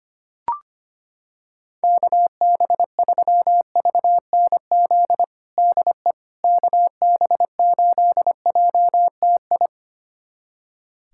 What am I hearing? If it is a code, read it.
Morse code, heard as a keyed tone, about 25 wpm, 700 Hz. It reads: KB3VNZ DE KB8JTI
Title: USA Callsign DE Callsign 25wpm